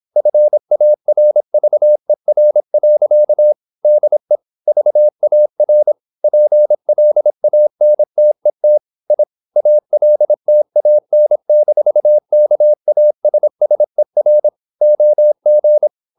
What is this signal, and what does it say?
Morse code, heard as a keyed tone, about 26 wpm, 600 Hz. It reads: FARVER. DE VAR PLANTET I ALTAN-KASSER OG
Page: Ferie 26 wpm